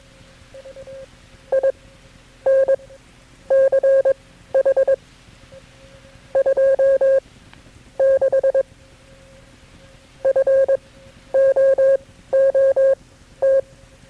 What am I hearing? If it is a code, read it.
Morse code, heard as a keyed tone, about 22 wpm, spaced as for 11 wpm, 540 Hz. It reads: INCH 26 FOOT
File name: KB6MT-tape3-side1